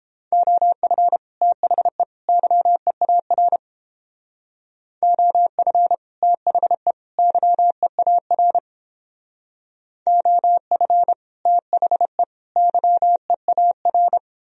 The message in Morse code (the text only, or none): OF THE YEAR OF THE YEAR OF THE YEAR